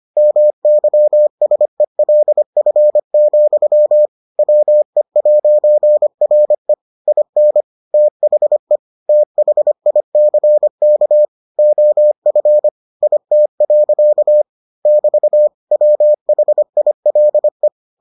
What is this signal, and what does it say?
Morse code, heard as a keyed tone, about 25 wpm, 600 Hz. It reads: MYSELF, WE'RE IN THE THICK OF IT. <BT> WHILE